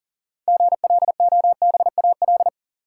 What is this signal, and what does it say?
Morse code, heard as a keyed tone, about 40 wpm, 700 Hz. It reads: GLOBAL
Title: Sentences from Top 1200 Words - New Word Drill Alt - 40wpm